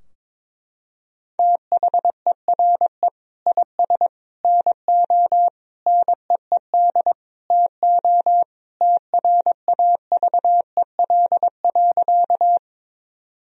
Morse code, heard as a keyed tone, about 22 wpm, 700 Hz. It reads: THERE IS NO NEED TO TRAVEL.
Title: Sentences from Top 1000 Words - Review Part 2 - 22wpm